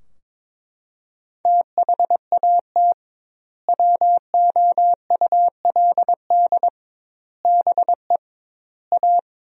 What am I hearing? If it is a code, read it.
Morse code, heard as a keyed tone, about 22 wpm, 700 Hz. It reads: THAT WOULD BE A
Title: 2x Word-Spacing - Sentences from Top 500 Words - 22wpm